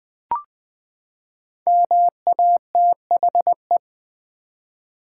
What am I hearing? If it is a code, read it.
Morse code, heard as a keyed tone, about 21 wpm, 700 Hz. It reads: MATHE
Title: Build A Word - 5 Letters At A Time - 12-Letter Words - 20wpm